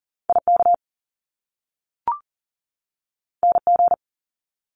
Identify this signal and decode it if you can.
Morse code, heard as a keyed tone, about 41 wpm, 700 Hz. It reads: IK NG
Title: Top Two Letter Combinations in Words 40wpm